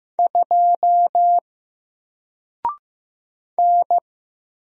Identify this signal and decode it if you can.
Morse code, heard as a keyed tone, about 15 wpm, 700 Hz. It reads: 2 N
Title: Single Letter-Number Alt 15wpm